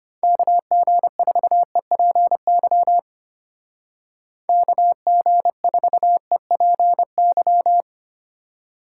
Morse code, heard as a keyed tone, about 29 wpm, 700 Hz. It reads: KG4EPY KG4EPY
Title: Speed Racing - USA 2x3 Call Signs 25wpm